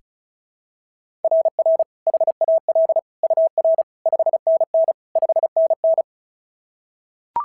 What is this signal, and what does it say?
Morse code, heard as a keyed tone, about 36 wpm, 690 Hz. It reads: RR HAL UR 5NN 5NN
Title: SOTA QSO Practice 35wpm